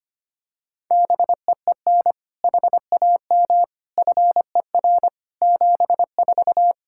Morse code, heard as a keyed tone, about 25 wpm, 700 Hz. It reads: BEEN HAM FER 74
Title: K1USN SST QSO Practice 25wpm